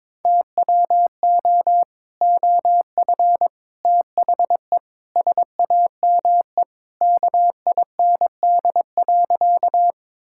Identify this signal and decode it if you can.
Morse code, heard as a keyed tone, about 22 wpm, 700 Hz. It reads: TWO OF THE SAME KIND.
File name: practice / 22wpm / Sentences with 3-Word Phrases - 22wpm